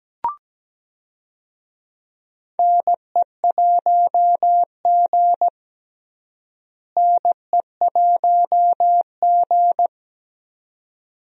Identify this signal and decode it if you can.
Morse code, heard as a keyed tone, about 17 wpm, 700 Hz. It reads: NE1G NE1G
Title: USA Call Signs - 4 Characters - Repeated 2x - 17wpm